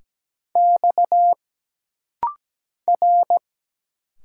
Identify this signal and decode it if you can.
Morse code, heard as a keyed tone, about 17 wpm, 700 Hz. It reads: X R
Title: Single Letters - Rapid-Fire 17wpm